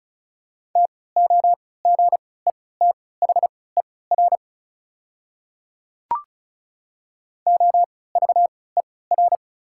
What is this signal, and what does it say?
Morse code, heard as a keyed tone, about 34 wpm, spaced as for 20 wpm, 690 Hz. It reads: TOGETHER OVER
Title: 3x Character Spacing - Top 300 Words - 35wpm